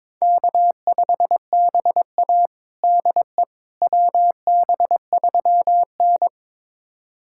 Morse code, heard as a keyed tone, about 22 wpm, 700 Hz. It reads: K5BA DE WB3N
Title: USA Callsign DE Callsign - 4 Characters - 22wpm